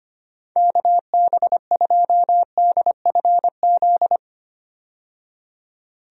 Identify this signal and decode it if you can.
Morse code, heard as a keyed tone, about 25 wpm, 700 Hz. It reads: KB2DFZ
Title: USA Call Signs - 6 Characters - 25wpm